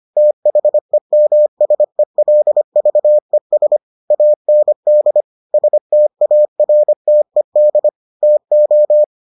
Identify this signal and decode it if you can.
Morse code, heard as a keyed tone, about 25 wpm, 600 Hz. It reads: THEMSELVES AND STARTED TO